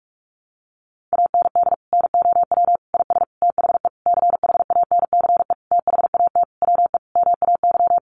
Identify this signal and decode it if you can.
Morse code, heard as a keyed tone, about 44 wpm, 700 Hz. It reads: AND NOW IS THE CHANCE THAT WE MAY
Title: Sentences from Top 800 Words 45wpm